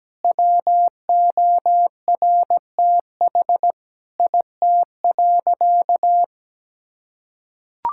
Morse code, heard as a keyed tone, about 17 wpm, 700 Hz. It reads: WORTH IT.
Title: Sentences from Top 1000 Words - Review Part 5 - 17wpm